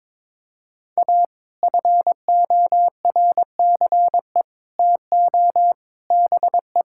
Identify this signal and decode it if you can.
Morse code, heard as a keyed tone, about 22 wpm, 700 Hz. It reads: A FORCE TO BE
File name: practice / 22wpm / Sets of 8 Words - 22wpm